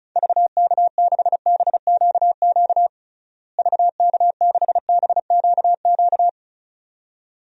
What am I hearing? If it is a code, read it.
Morse code, heard as a keyed tone, about 36 wpm, 700 Hz. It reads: VK6BQQ VK6BQQ
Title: DXCC-100 Call Signs - 6 Characters - Repeated 2x - 35wpm